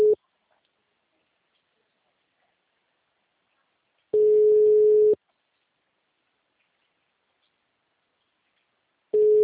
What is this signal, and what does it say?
Ringback tone (Europe, 425 Hz).
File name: out- -2005- - - .15710